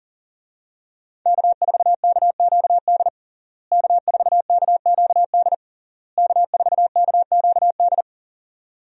Morse code, heard as a keyed tone, about 40 wpm, 700 Hz. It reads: K4KQD K4KQD K4KQD
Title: USA Call Signs - 5 Characters - Repeated 3x - 40wpm